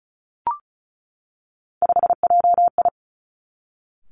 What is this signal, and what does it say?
Morse code, heard as a keyed tone, about 36 wpm, 700 Hz. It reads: 5JI
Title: Three Challenging Letter-Number Combinations 35wpm